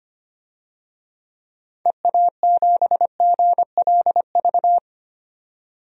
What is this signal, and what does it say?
Morse code, heard as a keyed tone, about 25 wpm, 700 Hz. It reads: EA7GLV
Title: DXCC-100 Call Signs - 6 Characters - 25wpm